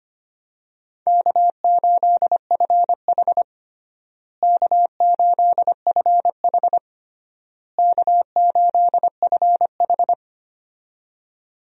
Morse code, heard as a keyed tone, about 25 wpm, 700 Hz. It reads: K8FH K8FH K8FH
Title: USA Call Signs - 4 Characters - Repeated 3x - 25wpm